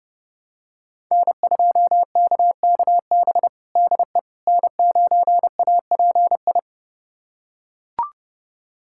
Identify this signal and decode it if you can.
Morse code, heard as a keyed tone, about 30 wpm, 700 Hz. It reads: N2KKB DE N9API
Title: USA Callsign DE Callsign - 5 Characters - 30wpm